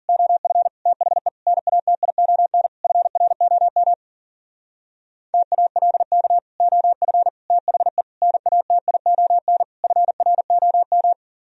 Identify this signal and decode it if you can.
Morse code, heard as a keyed tone, about 51 wpm, 700 Hz. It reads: OF THE NATION FROM TALK OF THE NATION FROM